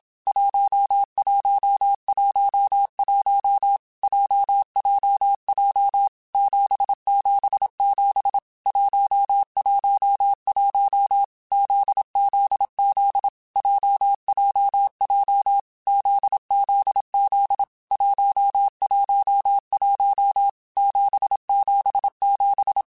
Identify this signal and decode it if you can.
Morse code, heard as a keyed tone, about 27 wpm, 800 Hz. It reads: 1111 JJJ 777 111 ZZZ JJJ ZZZ 111 777
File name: Groups@25 / Groups@25 mp3s / JZ17@25